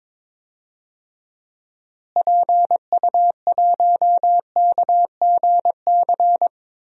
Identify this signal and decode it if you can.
Morse code, heard as a keyed tone, about 22 wpm, 700 Hz. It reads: PU1KGC
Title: DXCC-100 Call Signs - 6 Characters - Repeated 3x - 22wpm